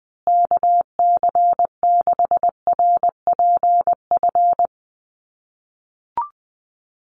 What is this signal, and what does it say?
Morse code, heard as a keyed tone, about 20 wpm, 700 Hz. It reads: KC6RPF
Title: USA Call Signs Repeated 3x 20wpm